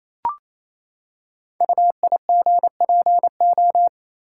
Morse code, heard as a keyed tone, about 27 wpm, 700 Hz. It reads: UIGPO
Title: Code Groups - 5 Letters - 28wpm